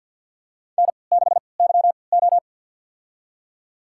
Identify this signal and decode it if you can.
Morse code, heard as a keyed tone, about 52 wpm, spaced as for 29 wpm, 700 Hz. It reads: N6YO
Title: USA Call Signs - 4 Characters - 3x Character Spacing - 50wpm